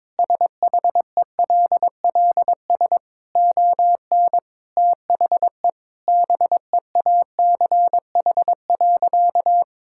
Morse code, heard as a keyed tone, about 22 wpm, 700 Hz. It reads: SHELLS ON THE BEACH.
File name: practice / 22wpm / Easy Sentences with Hints - 22wpm